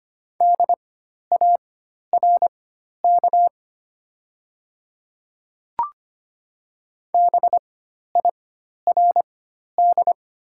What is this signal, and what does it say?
Morse code, heard as a keyed tone, about 25 wpm, spaced as for 12 wpm, 700 Hz. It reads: DARK BIRD
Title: 4x Character Spacing - Top 500 Words - 25wpm